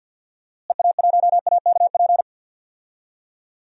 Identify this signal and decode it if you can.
Morse code, heard as a keyed tone, about 48 wpm, 700 Hz. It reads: EA1AKP
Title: DXCC-100 Call Signs - 6 Characters - Repeated 3x - 50wpm